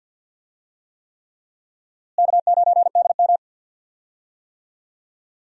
Morse code, heard as a keyed tone, about 47 wpm, 700 Hz. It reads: K9DM
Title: USA Call Signs - 4 Characters - Repeated 3x - 50wpm